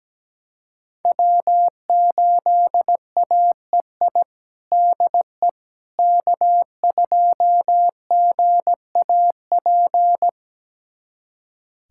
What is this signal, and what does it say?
Morse code, heard as a keyed tone, about 17 wpm, 700 Hz. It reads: W8AEI DE K2GAP
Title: USA Callsign DE Callsign - 5 Characters - 17wpm